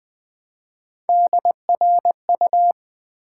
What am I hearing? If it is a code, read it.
Morse code, heard as a keyed tone, about 20 wpm, 700 Hz. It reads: DRU
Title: Build A Word - 1 Letter At A Time - 4-Letter Words - 20wpm